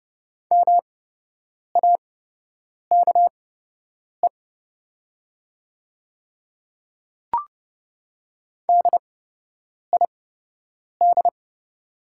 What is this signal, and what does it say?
Morse code, heard as a keyed tone, about 30 wpm, spaced as for 8 wpm, 700 Hz. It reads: MAKE DID
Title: 8x Character Spacing - Top 200 Words - 30wpm